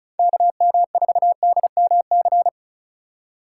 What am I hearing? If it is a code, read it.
Morse code, heard as a keyed tone, about 47 wpm, spaced as for 41 wpm, 700 Hz. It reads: KM4DMC
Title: Speed Racing - USA 2x3 Call Signs 35wpm